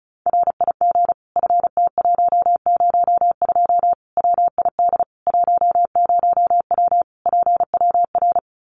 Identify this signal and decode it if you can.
Morse code, heard as a keyed tone, about 34 wpm, 700 Hz. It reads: RIG FT102 WID 10W PWR
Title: Common QSO Phrases 35wpm